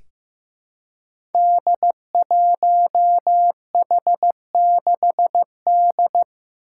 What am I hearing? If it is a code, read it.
Morse code, heard as a keyed tone, about 15 wpm, 700 Hz. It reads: D1H6D
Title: Five Challenging Letter-Number Combinations 15wpm